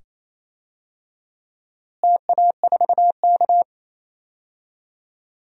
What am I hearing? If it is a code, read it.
Morse code, heard as a keyed tone, about 28 wpm, 700 Hz. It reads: TA4K
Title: DXCC-100 Call Signs - 4 Characters - Repeated 2x - 28wpm